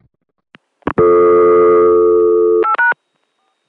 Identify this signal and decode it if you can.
Dial tone (North America), 350 + 440 Hz.